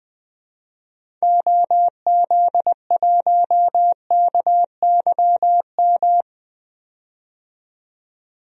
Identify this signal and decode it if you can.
Morse code, heard as a keyed tone, about 20 wpm, 700 Hz. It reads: OZ1KYM
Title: DXCC-100 Call Signs - 6 Characters - 20wpm